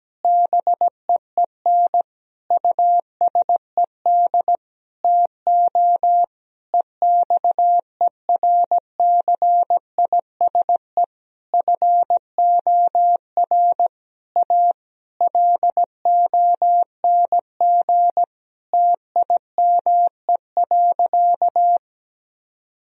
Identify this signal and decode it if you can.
Morse code, heard as a keyed tone, about 17 wpm, 700 Hz. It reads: BEEN USED TO EXERCISE FOR A LONG TIME.
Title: Sentences from Top 1000 Words - Review Part 4 - 17wpm